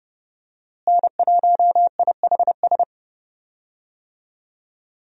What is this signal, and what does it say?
Morse code, heard as a keyed tone, about 30 wpm, 700 Hz. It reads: N1IHS
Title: USA Call Signs - 5 Characters - Repeated 3x - 30wpm